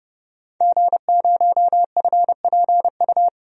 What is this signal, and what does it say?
Morse code, heard as a keyed tone, about 30 wpm, 700 Hz. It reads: G0FPU
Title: DXCC-100 Call Signs - 5 Characters - 30wpm